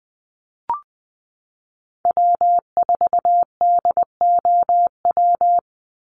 Morse code, heard as a keyed tone, about 19 wpm, 700 Hz. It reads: W4DOW
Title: NA SOTA Chaser Callsigns 20wpm